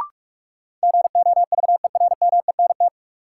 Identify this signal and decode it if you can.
Morse code, heard as a keyed tone, about 42 wpm, 700 Hz. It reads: GOVERMENT
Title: Sentences from Top 1300 Words - New Word Drill - 45wpm